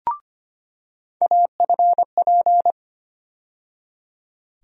Morse code, heard as a keyed tone, about 25 wpm, 700 Hz. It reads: AFP
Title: USA Call Sign Postfixes - 3 Letters 25wpm